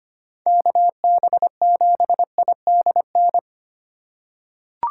Morse code, heard as a keyed tone, about 25 wpm, 700 Hz. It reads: KB7IDN
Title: USA Call Signs - 6 Characters - 3x Character Spacing - 25wpm